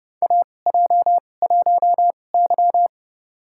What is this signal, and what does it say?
Morse code, heard as a keyed tone, about 30 wpm, spaced as for 22 wpm, 690 Hz. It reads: AJ1Y
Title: USA Call Signs - 4 Characters - 2x Character Spacing - 30wpm